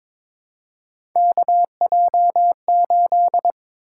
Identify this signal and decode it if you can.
Morse code, heard as a keyed tone, about 22 wpm, 700 Hz. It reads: KJ8
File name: practice / 22wpm / Three Letter-Number Combinations - 22wpm